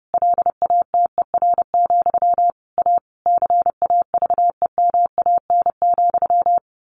Morse code, heard as a keyed tone, about 30 wpm, 700 Hz. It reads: LATER, A CAVEMAN,
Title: A Sound Of Thunder 30wpm